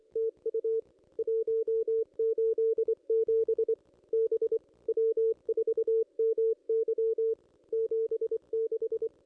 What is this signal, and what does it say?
Morse code, heard as a keyed tone, about 21 wpm, 440 Hz. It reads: TU 187 B W4MY 76
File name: contests / ss-cw-k5tr-op-n5rz -1500